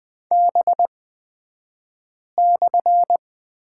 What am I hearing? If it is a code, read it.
Morse code, heard as a keyed tone, about 20 wpm, spaced as for 5 wpm, 700 Hz. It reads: B/
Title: 223 Lesson 36d - BT - Two Characters 20wpm